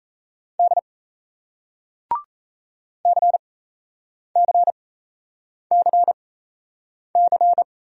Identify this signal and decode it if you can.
Morse code, heard as a keyed tone, about 38 wpm, spaced as for 8 wpm, 700 Hz. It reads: D CCCC